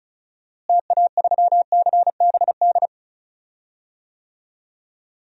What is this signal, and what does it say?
Morse code, heard as a keyed tone, about 36 wpm, 700 Hz. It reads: TA3CBD
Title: DXCC-100 Call Signs - 6 Characters - Repeated 3x - 35wpm